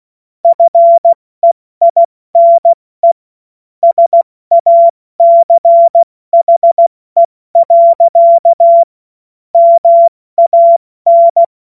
Morse code, heard as a keyed tone, about 16 wpm, 680 Hz. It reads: FEINE SACHE. MAN
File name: Morsekurs / Witze / Witz 08. Februar 2022 70 F80